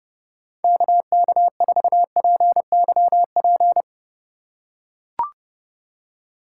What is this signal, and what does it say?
Morse code, heard as a keyed tone, about 30 wpm, 700 Hz. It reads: KK4PYP